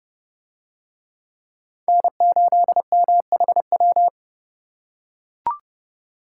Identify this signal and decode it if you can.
Morse code, heard as a keyed tone, about 28 wpm, 700 Hz. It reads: N8MHW